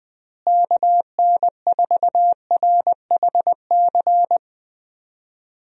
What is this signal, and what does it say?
Morse code, heard as a keyed tone, about 20 wpm, 700 Hz. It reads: KN4RHC
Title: USA Call Signs - 6 Characters - Repeated 3x - 20wpm